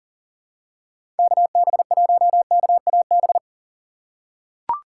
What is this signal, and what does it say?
Morse code, heard as a keyed tone, about 37 wpm, 700 Hz. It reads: KB1KAB
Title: USA Call Signs - 6 Characters - 3x Character Spacing - 40wpm